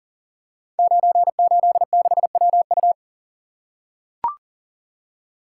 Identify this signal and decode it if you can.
Morse code, heard as a keyed tone, about 38 wpm, 690 Hz. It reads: 986WU
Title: Code Groups - 5 Letter:Numbers - 40wpm